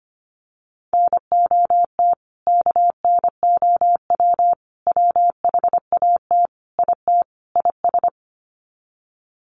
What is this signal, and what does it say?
Morse code, heard as a keyed tone, about 25 wpm, 700 Hz. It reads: NOT KNOW WHAT IT IS
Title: Sets of 5 Words Encore 25wpm